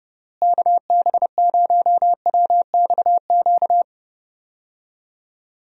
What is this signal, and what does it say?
Morse code, heard as a keyed tone, about 29 wpm, 700 Hz. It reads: KB0WXQ
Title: USA Call Signs - 6 Characters - Repeated 2x - 30wpm